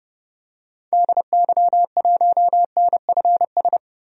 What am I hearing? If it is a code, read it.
Morse code, heard as a keyed tone, about 30 wpm, 700 Hz. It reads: DY1NFS
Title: DXCC-100 Call Signs - 6 Characters - 30wpm